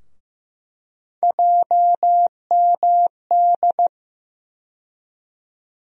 Morse code, heard as a keyed tone, about 16 wpm, 700 Hz. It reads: JMD
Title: USA Call Sign Postfixes - 3 Letters 15wpm